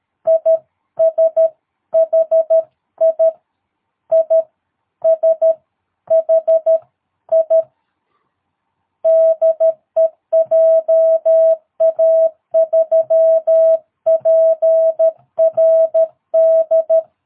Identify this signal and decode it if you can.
Morse code, heard as a keyed tone, about 13 wpm, spaced as for 11 wpm, 660 Hz. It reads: ISHI ISHI DE JA3PRD
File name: jh8cbh / 2022cw